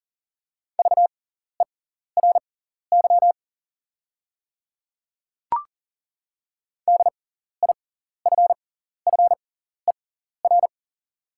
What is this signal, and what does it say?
Morse code, heard as a keyed tone, about 40 wpm, spaced as for 14 wpm, 700 Hz. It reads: VERY DIFFER
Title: 6x Character Spacing - Top 300 Words - 40wpm